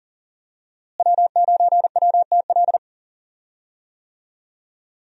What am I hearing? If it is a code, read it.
Morse code, heard as a keyed tone, about 40 wpm, 700 Hz. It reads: W9WTL
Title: USA Call Signs - 5 Characters - Repeated 2x - 40wpm